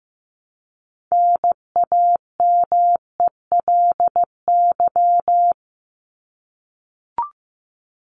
Morse code, heard as a keyed tone, about 15 wpm, 700 Hz. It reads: NAMELY
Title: LY Suffix - Top 100 Words with Base Words 15wpm